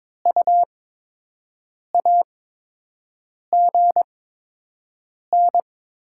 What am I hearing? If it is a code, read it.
Morse code, heard as a keyed tone, about 22 wpm, 700 Hz. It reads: U A G N
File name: Top 20 QSO Elements - 8x Character Spacing - 22wpm